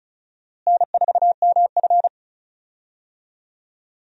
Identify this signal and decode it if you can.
Morse code, heard as a keyed tone, about 34 wpm, 700 Hz. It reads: N4MF
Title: USA Call Signs - 4 Characters - Repeated 3x - 35wpm